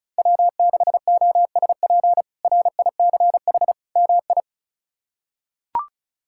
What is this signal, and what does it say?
Morse code, heard as a keyed tone, about 34 wpm, 700 Hz. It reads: W6OSP RICH MI
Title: K1USN SST QSO Practice - 35wpm